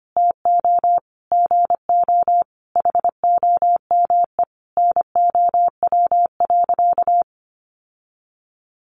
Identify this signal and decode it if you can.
Morse code, heard as a keyed tone, about 25 wpm, 700 Hz. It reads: TO GO HOME NOW.
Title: Sentences from Top 1300 Words 25wpm